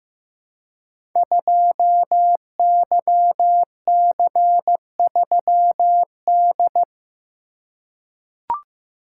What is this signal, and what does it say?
Morse code, heard as a keyed tone, about 15 wpm, 700 Hz. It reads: 2YC3D
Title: Code Groups - 5 Letter:Numbers - 15wpm